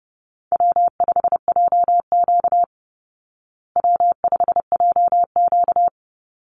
Morse code, heard as a keyed tone, about 30 wpm, 700 Hz. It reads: W5JQ W5JQ
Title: Easy Call Signs Repeated 2x - 30wpm